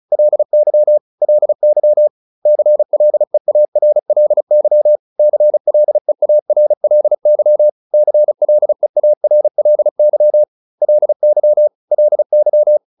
Morse code, heard as a keyed tone, about 34 wpm, 600 Hz. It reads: LY LY CLEARLY CLEARLY CLEARLY LY LY
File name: LY7 35